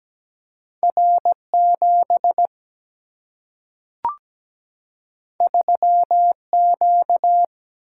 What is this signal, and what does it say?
Morse code, heard as a keyed tone, about 17 wpm, 700 Hz. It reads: R7 3Q
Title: Single-Letter Single-Number 17wpm